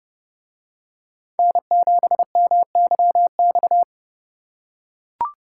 Morse code, heard as a keyed tone, about 30 wpm, 700 Hz. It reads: N7MYX